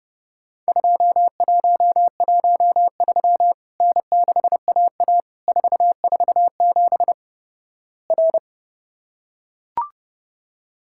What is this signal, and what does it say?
Morse code, heard as a keyed tone, about 29 wpm, 700 Hz. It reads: 2113 N6AA 447 R
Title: Worked All Europe DX Contest - QTC Exchange - 30wpm